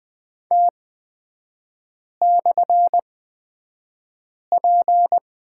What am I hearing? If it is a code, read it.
Morse code, heard as a keyed tone, about 20 wpm, spaced as for 5 wpm, 690 Hz. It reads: T/P
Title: 255 Lesson 41e - Period - Three Characters 20wpm